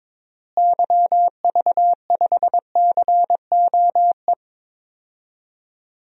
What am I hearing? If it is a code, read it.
Morse code, heard as a keyed tone, about 22 wpm, 700 Hz. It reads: YV5COE
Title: DXCC-100 Call Signs - 6 Characters - Repeated 3x - 22wpm